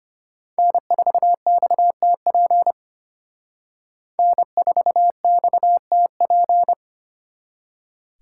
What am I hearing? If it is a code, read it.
Morse code, heard as a keyed tone, about 33 wpm, 700 Hz. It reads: N4XTP N4XTP